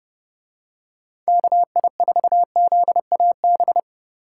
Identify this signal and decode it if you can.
Morse code, heard as a keyed tone, about 30 wpm, 700 Hz. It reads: KI4ZAB